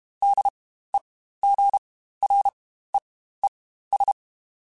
Morse code, heard as a keyed tone, about 32 wpm, spaced as for 15 wpm, 800 Hz. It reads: DEGREES
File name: Spaced Characters / Spaced Characters-mp3s / Weather5 22 sp @30